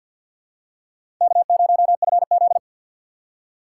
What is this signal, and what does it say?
Morse code, heard as a keyed tone, about 47 wpm, 700 Hz. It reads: K0FZ
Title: USA Call Signs - 4 Characters - 6x Character Spacing - 50wpm